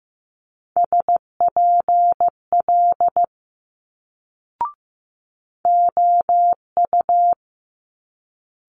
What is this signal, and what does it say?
Morse code, heard as a keyed tone, about 15 wpm, 700 Hz. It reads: SPL OU
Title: Multi-letter Phonemes Quick Fire 15wpm